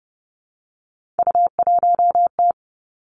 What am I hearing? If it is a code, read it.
Morse code, heard as a keyed tone, about 30 wpm, 700 Hz. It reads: U1T
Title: Three Challenging Letter-Number Combinations 30wpm